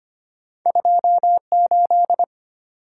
Morse code, heard as a keyed tone, about 34 wpm, spaced as for 30 wpm, 700 Hz. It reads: 28
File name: practice / 25wpm / Speed-Racing - Two Letter-Number Combinations - 25wpm